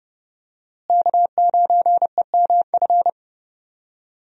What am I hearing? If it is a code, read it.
Morse code, heard as a keyed tone, about 32 wpm, 700 Hz. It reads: K9EMF
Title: USA Call Signs - 5 Characters - Repeated 2x - 30wpm